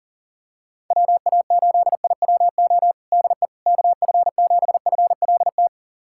Morse code, heard as a keyed tone, about 40 wpm, 700 Hz. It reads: WA8IWO DE KF7FLT
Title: USA Callsign DE Callsign 40wpm